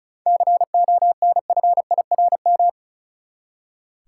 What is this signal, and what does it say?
Morse code, heard as a keyed tone, about 36 wpm, 700 Hz. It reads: CONFIRM
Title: Seven Letter Words - 35wpm